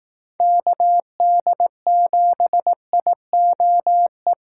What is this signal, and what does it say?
Morse code, heard as a keyed tone, about 18 wpm, 700 Hz. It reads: KD7IOE
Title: Speed Racing - USA 2x3 Call Signs 15wpm